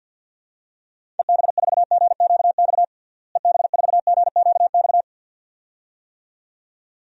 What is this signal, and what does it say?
Morse code, heard as a keyed tone, about 53 wpm, 700 Hz. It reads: EB4GQX EB4GQX
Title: DXCC-100 Call Signs - 6 Characters - Repeated 2x - 50wpm